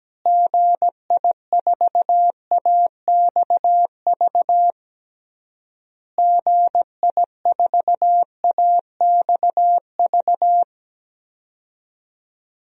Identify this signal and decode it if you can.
Morse code, heard as a keyed tone, about 17 wpm, 700 Hz. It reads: GI4AXV GI4AXV
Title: DXCC-100 Call Signs - 6 Characters - Repeated 3x - 17wpm